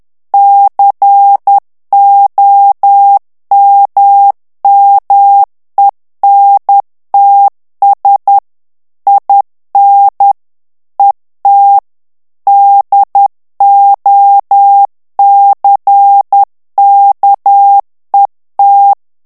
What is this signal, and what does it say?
Morse code, heard as a keyed tone, about 10 wpm, 800 Hz. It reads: COMMENTS IN ET DOCKET
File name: ANGLAIS texte clair 3 vitesse 10 mots minute